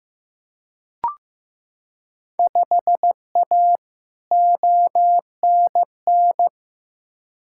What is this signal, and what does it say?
Morse code, heard as a keyed tone, about 15 wpm, 700 Hz. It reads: 5A ONN
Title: Field Day Exchanges Encore - 15wpm